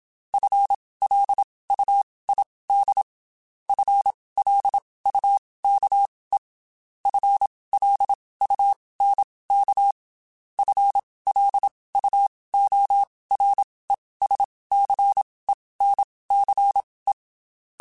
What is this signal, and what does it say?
Morse code, heard as a keyed tone, about 27 wpm, spaced as for 19 wpm, 800 Hz. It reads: FLUID FLUKE FLUNK FLUORESCENCE
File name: Spaced / FL Runs sp mp3s / FL Runs sp @25